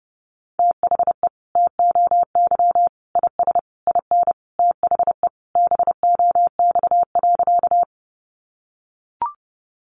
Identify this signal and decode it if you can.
Morse code, heard as a keyed tone, about 30 wpm, 700 Hz. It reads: THE TOY IS IN THE BOX.
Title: Easy Phrases 30wpm